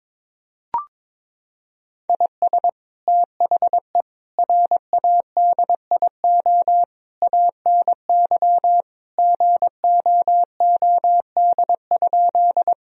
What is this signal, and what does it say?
Morse code, heard as a keyed tone, about 22 wpm, 700 Hz. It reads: IS THE RADIO ANY GOOD?
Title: Sentences from Top 1000 Words - Review Part 3 - 22wpm